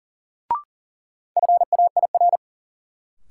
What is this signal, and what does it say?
Morse code, heard as a keyed tone, about 37 wpm, 700 Hz. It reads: FAIR
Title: Sentences from Top 1300 Words - New Word Drill Alt - 40wpm